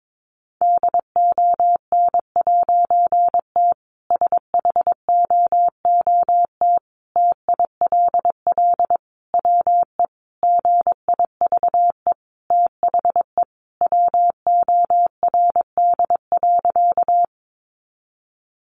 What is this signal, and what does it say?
Morse code, heard as a keyed tone, about 22 wpm, 700 Hz. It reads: DON'T SHOOT TILL WE GIVE THE WORD.
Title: A Sound Of Thunder 22wpm